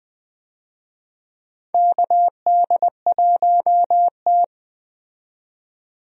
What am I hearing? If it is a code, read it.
Morse code, heard as a keyed tone, about 20 wpm, 700 Hz. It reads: KD1T
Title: USA Call Signs - 4 Characters - 20wpm